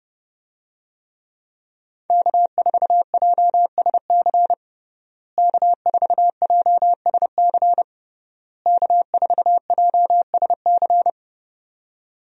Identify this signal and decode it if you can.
Morse code, heard as a keyed tone, about 30 wpm, 700 Hz. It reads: K4JSC K4JSC K4JSC
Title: USA Call Signs - 5 Characters - Repeated 3x - 30wpm